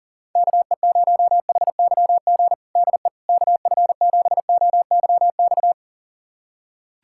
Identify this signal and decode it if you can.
Morse code, heard as a keyed tone, about 40 wpm, 700 Hz. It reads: KE0HYG DE KF7OYX
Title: USA Callsign DE Callsign - 6 Characters - 40wpm